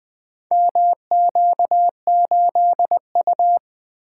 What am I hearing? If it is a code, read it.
Morse code, heard as a keyed tone, about 20 wpm, 700 Hz. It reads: MQ8U
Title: Four Letter-Number Combinations - 20wpm